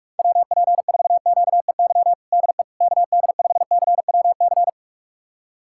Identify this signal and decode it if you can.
Morse code, heard as a keyed tone, about 42 wpm, 700 Hz. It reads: WP4QEY DE KD5CWC
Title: USA Callsign DE Callsign 45wpm